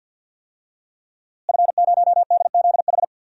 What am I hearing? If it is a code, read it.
Morse code, heard as a keyed tone, about 51 wpm, 700 Hz. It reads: F0DZH
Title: DXCC-100 Call Signs - 5 Characters - Repeated 2x - 50wpm